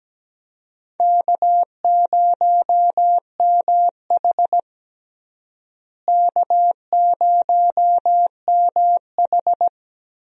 Morse code, heard as a keyed tone, about 17 wpm, 700 Hz. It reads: K0MH K0MH
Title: USA Call Signs - 4 Characters - Repeated 3x - 17wpm